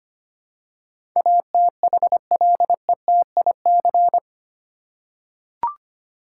Morse code, heard as a keyed tone, about 25 wpm, 700 Hz. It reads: ATHLETIC
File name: Eight Letter Words - 25wpm